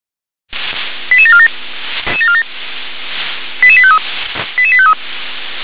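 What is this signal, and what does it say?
Five-tone selective call (ZVEI), heard as quick runs of five tones, tones of about 70 ms. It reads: 88747 88853 88853
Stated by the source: ZVEI five-tone selective call received by SDR.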